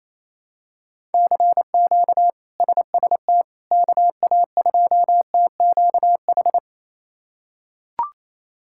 Morse code, heard as a keyed tone, about 28 wpm, 700 Hz. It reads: CQ SST KA2TQH
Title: K1USN SST QSO Practice - 28wpm